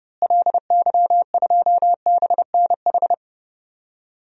Morse code, heard as a keyed tone, about 30 wpm, 700 Hz. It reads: LY2BNH
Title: DXCC-100 Call Signs - 6 Characters - Repeated 3x - 30wpm